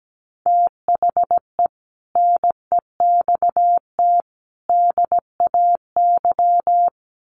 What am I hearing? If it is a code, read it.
Morse code, heard as a keyed tone, about 17 wpm, 700 Hz. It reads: THE NEXT DAY
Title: Sets of 3 Words 17wpm